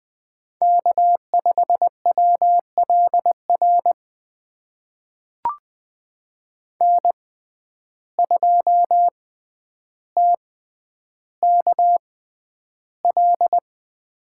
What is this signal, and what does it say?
Morse code, heard as a keyed tone, about 20 wpm, spaced as for 10 wpm, 700 Hz. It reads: K5WLR N2TKL
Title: USA Call Signs - 5 Characters - 6x Character Spacing - 20wpm